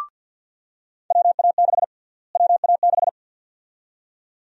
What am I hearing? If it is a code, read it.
Morse code, heard as a keyed tone, about 51 wpm, 700 Hz. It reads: WA6 WA6
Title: USA Call Sign Prefixes - Repeated 2x - 50wpm